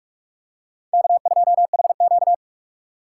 Morse code, heard as a keyed tone, about 44 wpm, 700 Hz. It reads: K2HQ